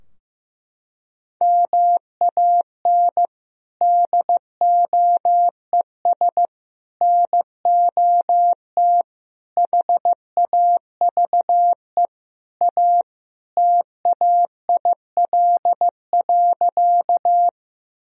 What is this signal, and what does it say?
Morse code, heard as a keyed tone, about 15 wpm, 700 Hz. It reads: MAN DOES NOT HAVE A TAIL.
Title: Sentences from Top 1000 Words - Review Part 3 - 15wpm